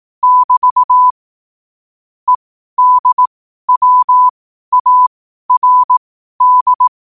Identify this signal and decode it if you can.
Morse code, heard as a keyed tone, about 18 wpm, spaced as for 12 wpm, 1000 Hz. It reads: <BT> EDWARD
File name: qotdfiles / 12wpm